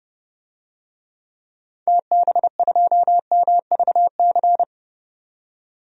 Morse code, heard as a keyed tone, about 31 wpm, 700 Hz. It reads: TB2MVC
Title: DXCC-100 Call Signs - 6 Characters - 30wpm